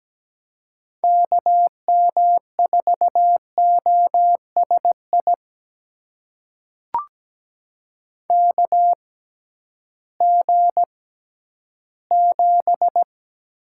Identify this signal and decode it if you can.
Morse code, heard as a keyed tone, about 17 wpm, spaced as for 11 wpm, 700 Hz. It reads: KM4OSI KG7
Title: USA Call Signs - 6 Characters - 6x Character Spacing - 17wpm